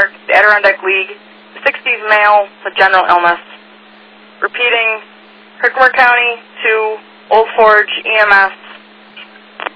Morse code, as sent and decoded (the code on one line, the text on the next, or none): none